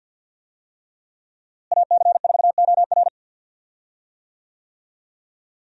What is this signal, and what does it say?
Morse code, heard as a keyed tone, about 46 wpm, 690 Hz. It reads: AC4OR
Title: USA Call Signs - 5 Characters - Repeated 3x - 50wpm